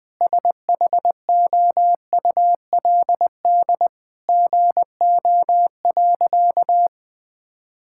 Morse code, heard as a keyed tone, about 20 wpm, 700 Hz. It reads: SHOULD GO.
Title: Sentences with 4-Word Phrases - 20wpm